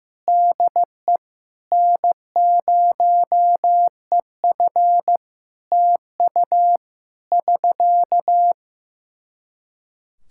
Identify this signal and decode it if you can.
Morse code, heard as a keyed tone, about 15 wpm, 700 Hz. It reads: DE N0EF TU <SK>
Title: Practice QSO - Exchange 4 - Ending - 15wpm